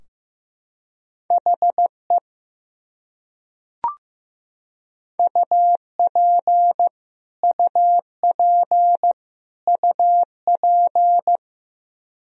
Morse code, heard as a keyed tone, about 15 wpm, 700 Hz. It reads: HE UP UP UP
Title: Top 100 Words Repeated 15wpm